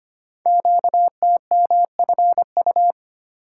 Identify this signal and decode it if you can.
Morse code, heard as a keyed tone, about 25 wpm, 700 Hz. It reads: QTMFU
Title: Code Groups - 5 Letters - 25wpm